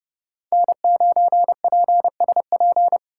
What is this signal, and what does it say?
Morse code, heard as a keyed tone, about 29 wpm, 700 Hz. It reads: N9PSP